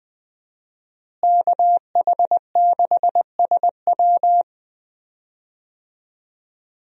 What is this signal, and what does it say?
Morse code, heard as a keyed tone, about 20 wpm, 700 Hz. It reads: KH6SW
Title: DXCC-100 Call Signs - 5 Characters - Repeated 2x - 20wpm